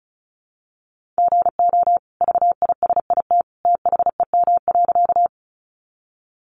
Morse code, heard as a keyed tone, about 34 wpm, 700 Hz. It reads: GO VISIT THEM.
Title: Sentences from Top 900 Words 35wpm